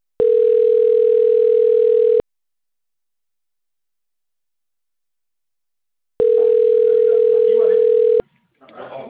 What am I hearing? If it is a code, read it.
Ringback tone (North America), 440 + 480 Hz, with speech.
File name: out- - - - - .21911